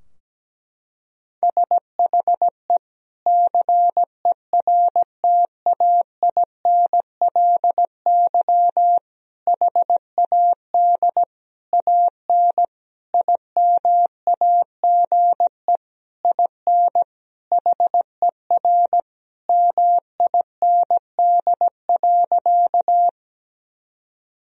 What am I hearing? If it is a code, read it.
Morse code, heard as a keyed tone, about 17 wpm, 700 Hz. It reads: SHE CERTAINLY HAD AN IMAGE IN HER MIND.
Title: Sentences from Top 1000 Words - Review Part 4 - 17wpm